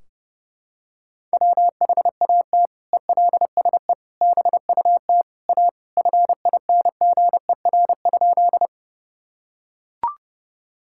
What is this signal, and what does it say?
Morse code, heard as a keyed tone, about 30 wpm, 700 Hz. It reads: WHAT ELSE BUT A FINGER?
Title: Sentences from Top 1600 Words 30wpm